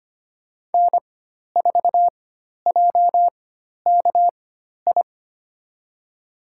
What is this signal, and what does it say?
Morse code, heard as a keyed tone, about 25 wpm, spaced as for 12 wpm, 700 Hz. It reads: N4JKI
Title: USA Call Signs - 5 Characters - 4x Character Spacing - 25wpm